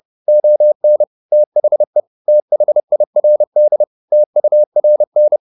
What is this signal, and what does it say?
Morse code, heard as a keyed tone, about 30 wpm, 600 Hz. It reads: ON THE THIRD TURN